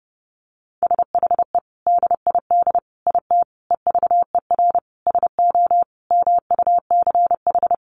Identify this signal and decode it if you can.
Morse code, heard as a keyed tone, about 30 wpm, 700 Hz. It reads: SHE DID IT EVER SO MUCH
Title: Sentences from Top 400 Words 30wpm